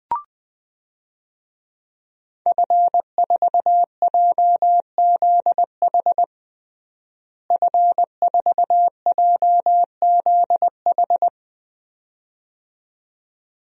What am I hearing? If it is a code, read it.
Morse code, heard as a keyed tone, about 21 wpm, 700 Hz. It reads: F4JZH F4JZH
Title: DXCC-100 Call Signs - 5 Characters - Repeated 2x - 20wpm